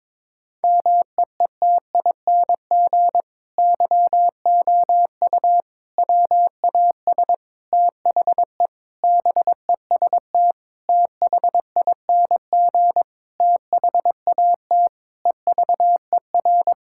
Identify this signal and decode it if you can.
Morse code, heard as a keyed tone, about 22 wpm, 700 Hz. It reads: MEETING YOU WAS THE BEST THING THAT EVER
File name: practice / 22wpm / Sentences with 8-Word Phrases - 22wpm